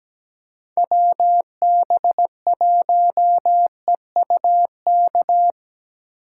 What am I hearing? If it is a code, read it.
Morse code, heard as a keyed tone, about 20 wpm, 700 Hz. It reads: WB1EUK
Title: Speed Racing - USA 2x3 Call Signs 17wpm